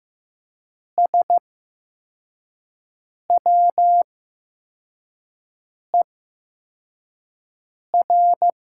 Morse code, heard as a keyed tone, about 15 wpm, spaced as for 4 wpm, 700 Hz. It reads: SWER
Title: 8x Character Spacing - Top 300 Words - 15wpm